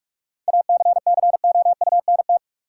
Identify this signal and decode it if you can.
Morse code, heard as a keyed tone, about 41 wpm, 700 Hz. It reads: ACCOUNT
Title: Sentences from Top 900 Words - New Word Drill - 45wpm